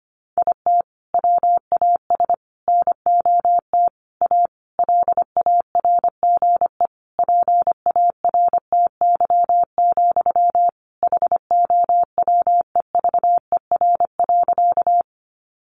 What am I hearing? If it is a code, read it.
Morse code, heard as a keyed tone, about 25 wpm, 700 Hz. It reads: IT WAS NOT A LARGE PARTY, HOWEVER.
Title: Sentences from Top 1400 Words 25wpm